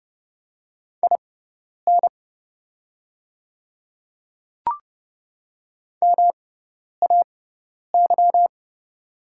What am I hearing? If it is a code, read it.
Morse code, heard as a keyed tone, about 30 wpm, spaced as for 10 wpm, 690 Hz. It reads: IN MAY
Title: 6x Character Spacing - Top 100 Words - 30wpm